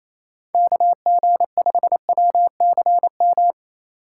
Morse code, heard as a keyed tone, about 28 wpm, 700 Hz. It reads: KG5WCM
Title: USA Call Signs - 6 Characters - 2x Character Spacing - 28wpm